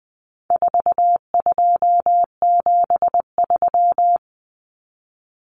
Morse code, heard as a keyed tone, about 20 wpm, 700 Hz. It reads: 4273
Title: Numbers - 4-Digit 20wpm